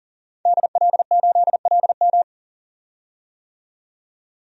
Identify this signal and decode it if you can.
Morse code, heard as a keyed tone, about 40 wpm, 700 Hz. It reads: DL8LM
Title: DXCC-100 Call Signs - 5 Characters - 40wpm